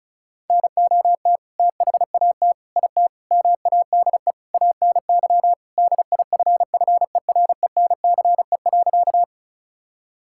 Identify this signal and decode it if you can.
Morse code, heard as a keyed tone, about 34 wpm, 700 Hz. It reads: NOT THAT IT MADE ANY DIFFERENCE.
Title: Sentences from Top 1000 Words - Review Part 5 - 35wpm